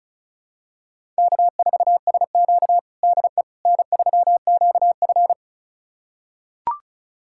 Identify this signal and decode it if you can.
Morse code, heard as a keyed tone, about 36 wpm, 700 Hz. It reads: K4SQ DE N3QF
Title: USA Callsign DE Callsign - 4 Characters - 35wpm